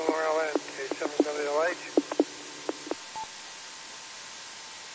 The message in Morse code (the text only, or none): none